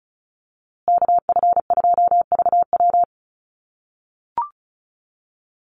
Morse code, heard as a keyed tone, about 34 wpm, 700 Hz. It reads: KF2VW
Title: USA Call Signs Repeated 3x 35wpm